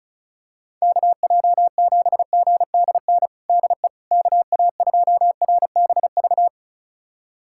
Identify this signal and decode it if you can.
Morse code, heard as a keyed tone, about 34 wpm, 700 Hz. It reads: KJ7GDN DE KA2RBV
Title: USA Callsign DE Callsign - 6 Characters - 35wpm